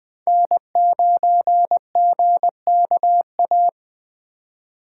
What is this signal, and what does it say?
Morse code, heard as a keyed tone, about 20 wpm, 700 Hz. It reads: N9GKA
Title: USA Call Signs - 5 Characters - 2x Character Spacing - 20wpm